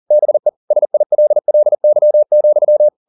Morse code, heard as a keyed tone, about 40 wpm, 600 Hz. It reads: BE SILLY,
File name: the old man and the sea 2 40wpm